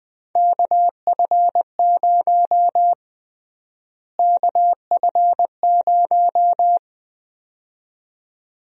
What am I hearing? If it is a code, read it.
Morse code, heard as a keyed tone, about 20 wpm, 700 Hz. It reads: KF0 KF0
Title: USA Call Sign Prefixes - Repeated 2x - 20wpm